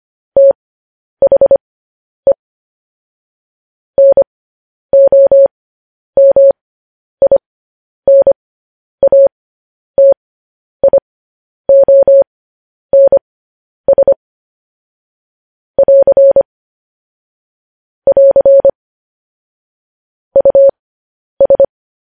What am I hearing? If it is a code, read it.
Morse code, heard as a keyed tone, about 25 wpm, spaced as for 10 wpm, 550 Hz. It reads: THE NOMINATIONS <AR> <AR> US